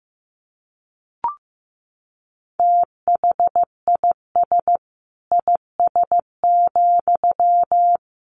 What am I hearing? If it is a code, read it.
Morse code, heard as a keyed tone, about 15 wpm, 700 Hz. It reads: THIS IS,
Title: Sentences from Top 1500 Words 15wpm